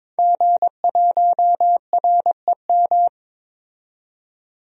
Morse code, heard as a keyed tone, about 22 wpm, 700 Hz. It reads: G1REM
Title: DXCC-100 Call Signs - 5 Characters - 22wpm